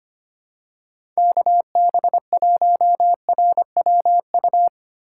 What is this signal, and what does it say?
Morse code, heard as a keyed tone, about 25 wpm, 700 Hz. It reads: KB1RWU
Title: USA Call Signs - 6 Characters - 3x Character Spacing - 25wpm